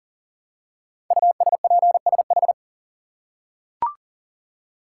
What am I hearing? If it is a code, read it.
Morse code, heard as a keyed tone, about 40 wpm, 700 Hz. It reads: USPSH
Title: Code Groups - 5 Letters - 40wpm